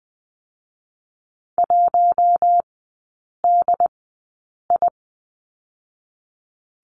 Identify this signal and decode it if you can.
Morse code, heard as a keyed tone, about 20 wpm, spaced as for 8 wpm, 700 Hz. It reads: 1DI
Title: 152 Lesson 25e - Letter p - Three Characters 20wpm